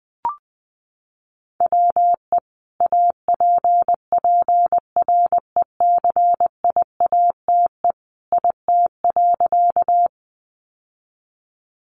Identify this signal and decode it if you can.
Morse code, heard as a keyed tone, about 20 wpm, 700 Hz. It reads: WE APPRECIATE IT.
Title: Sentences from Top 1300 Words 20wpm